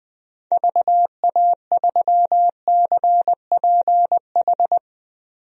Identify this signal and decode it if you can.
Morse code, heard as a keyed tone, about 20 wpm, 700 Hz. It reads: VA3CPH
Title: DXCC-100 Call Signs - 6 Characters - Repeated 3x - 20wpm